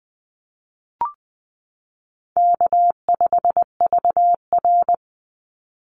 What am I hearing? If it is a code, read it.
Morse code, heard as a keyed tone, about 20 wpm, 700 Hz. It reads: K5VR
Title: Easy Call Signs - 20wpm